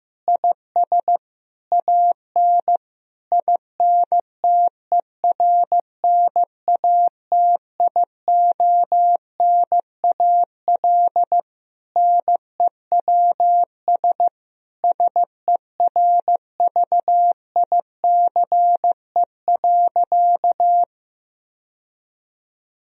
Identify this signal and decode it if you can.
Morse code, heard as a keyed tone, about 15 wpm, 700 Hz. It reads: IS AN INTERNATIONAL NEWS SERVICE.
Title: Sentences from Top 1000 Words - Review Part 4 - 15wpm